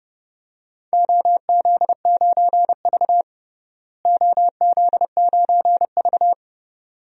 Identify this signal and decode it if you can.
Morse code, heard as a keyed tone, about 30 wpm, 700 Hz. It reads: OZ9V OZ9V
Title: DXCC-100 Call Signs - 4 Characters - Repeated 2x - 30wpm